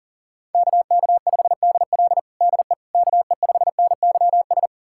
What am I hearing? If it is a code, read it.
Morse code, heard as a keyed tone, about 40 wpm, 700 Hz. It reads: KK5DL DE KE5NYS
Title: USA Callsign DE Callsign 40wpm